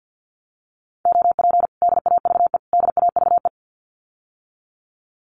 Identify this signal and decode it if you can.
Morse code, heard as a keyed tone, about 50 wpm, 700 Hz. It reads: OP DAVE DAVE
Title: Common QSO Phrases Encore 50wpm